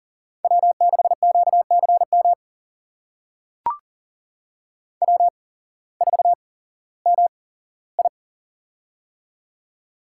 Morse code, heard as a keyed tone, about 39 wpm, spaced as for 19 wpm, 700 Hz. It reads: W6QCM W4MI
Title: USA Call Signs - 8x Character Spacing - 40wpm